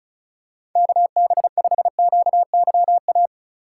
Morse code, heard as a keyed tone, about 36 wpm, 700 Hz. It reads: KB5QYA
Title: USA Call Signs - 2x Character Spacing - 35wpm